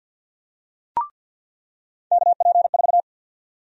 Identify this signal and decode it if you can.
Morse code, heard as a keyed tone, about 50 wpm, 700 Hz. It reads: KP4
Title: USA Call Sign Prefixes - 50wpm